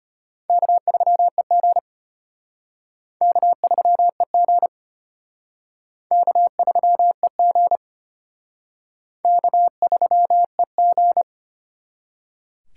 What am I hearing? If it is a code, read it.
Morse code, heard as a keyed tone, about 31 wpm, 700 Hz. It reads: K3EG K3EG K3EG K3EG